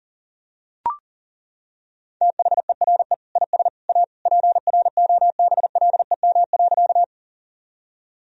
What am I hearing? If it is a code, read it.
Morse code, heard as a keyed tone, about 40 wpm, 700 Hz. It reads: THERE IS A PROBLEM.
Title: Sentences from Top 1000 Words - Review Part 2 - 40wpm